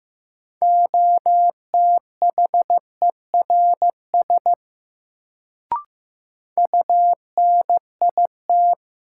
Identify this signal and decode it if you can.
Morse code, heard as a keyed tone, about 15 wpm, 700 Hz. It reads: OTHERS UNIT
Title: Sentences from Top 1000 Words - Review Part 2 - New Word Drill - 15wpm